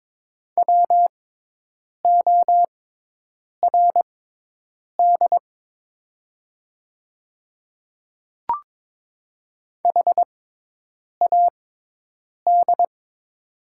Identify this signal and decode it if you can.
Morse code, heard as a keyed tone, about 22 wpm, spaced as for 8 wpm, 700 Hz. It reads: WORD HAD
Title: 6x Character Spacing - Top 100 Words - 22wpm